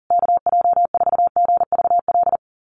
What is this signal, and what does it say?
Morse code, heard as a keyed tone, about 40 wpm, 700 Hz. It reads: KJ4GVL
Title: USA Call Signs Repeated 3x 40wpm